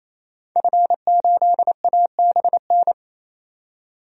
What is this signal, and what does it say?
Morse code, heard as a keyed tone, about 28 wpm, 700 Hz. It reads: F8ABN
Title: DXCC-100 Call Signs - 5 Characters - 28wpm